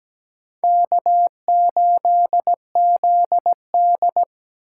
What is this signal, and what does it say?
Morse code, heard as a keyed tone, about 17 wpm, 700 Hz. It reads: K8ZD
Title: USA Call Signs - 3x Character Spacing - 17wpm